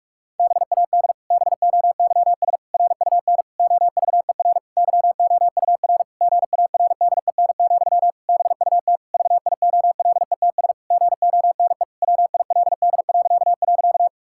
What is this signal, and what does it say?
Morse code, heard as a keyed tone, about 46 wpm, 700 Hz. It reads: BAD BOYS RUN OVER YOUR GARDEN, BUT VIOLETS GONE WILD..
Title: Pangram Challenge v2 - 45wpm